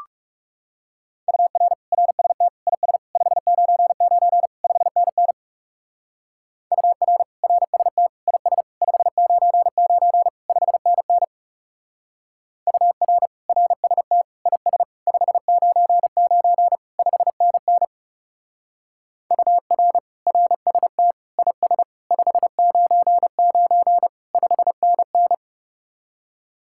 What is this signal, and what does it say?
Morse code, heard as a keyed tone, about 36 wpm, 700 Hz. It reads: UR RST IS 599 5NN UR RST IS 599 5NN UR RST IS 599 5NN UR RST IS 599 5NN
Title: Speed-Racing Common QSO Phrases - 30wpm